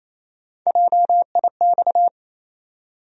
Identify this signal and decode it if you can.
Morse code, heard as a keyed tone, about 28 wpm, 700 Hz. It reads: JIX
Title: USA Call Sign Postfixes - 3 Letters 28wpm